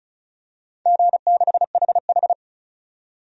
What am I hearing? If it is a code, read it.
Morse code, heard as a keyed tone, about 36 wpm, 700 Hz. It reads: G6HH
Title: DXCC-100 Call Signs - 4 Characters - 35wpm